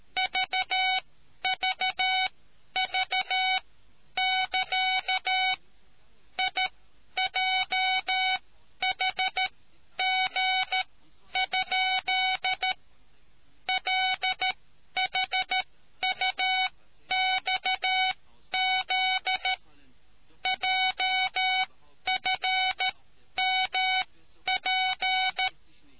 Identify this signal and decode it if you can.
Morse code, heard as a keyed tone, about 13 wpm, spaced as for 10 wpm, 750 Hz. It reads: VVV<KA> IJHG? LHUXZ JFMP